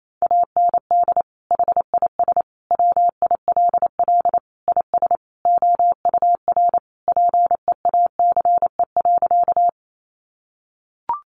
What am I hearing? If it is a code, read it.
Morse code, heard as a keyed tone, about 28 wpm, 700 Hz. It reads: AND HIS WILL IS OUR PEACE.
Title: Sentences from Top 1500 Words 28wpm